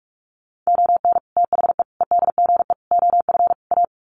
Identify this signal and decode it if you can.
Morse code, heard as a keyed tone, about 44 wpm, 700 Hz. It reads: ON THE EDGE OF A
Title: Sets of 5 Words Encore2 45wpm